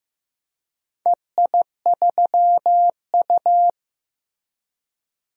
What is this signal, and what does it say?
Morse code, heard as a keyed tone, about 15 wpm, 700 Hz. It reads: EI3U
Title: DXCC-100 Call Signs - 4 Characters - Repeated 3x - 15wpm